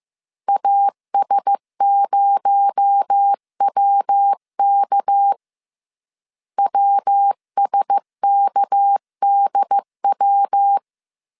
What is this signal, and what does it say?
Morse code, heard as a keyed tone, about 15 wpm, 800 Hz. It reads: AS0WK WSKDW
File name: CW / DET / 1 6 1500C S C 14wpm 2951